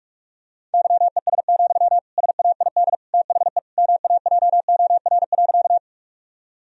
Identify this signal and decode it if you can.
Morse code, heard as a keyed tone, about 46 wpm, 700 Hz. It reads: YES, SAID THE MAJOR.
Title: Sentences from Top 1000 Words - Review Part 4 - 45wpm